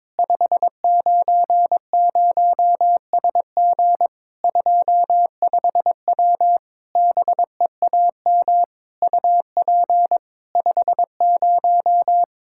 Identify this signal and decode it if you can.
Morse code, heard as a keyed tone, about 22 wpm, 700 Hz. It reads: 590SG 25W BEAM UP 50
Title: Practice QSO - Exchange 2 - Info - 22wpm